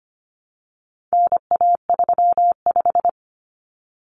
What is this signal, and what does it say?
Morse code, heard as a keyed tone, about 25 wpm, 700 Hz. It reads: NA35
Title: Common Cut Numbers - 4-Digit 25wpm